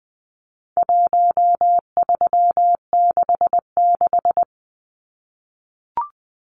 Morse code, heard as a keyed tone, about 20 wpm, 700 Hz. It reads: 1366
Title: Numbers - Contest Style - 4-Digit 20wpm